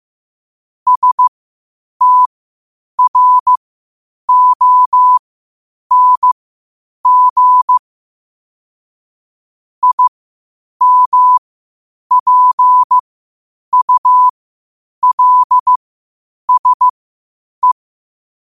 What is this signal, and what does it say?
Morse code, heard as a keyed tone, about 15 wpm, spaced as for 8 wpm, 1000 Hz. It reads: STRONG IMPULSE
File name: qotdfiles / 8wpm